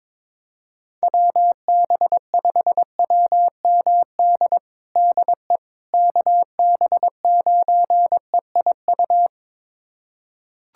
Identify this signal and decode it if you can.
Morse code, heard as a keyed tone, about 22 wpm, 700 Hz. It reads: WB5WMD DE KB9EIU
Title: USA Callsign DE Callsign - 6 Characters - 22wpm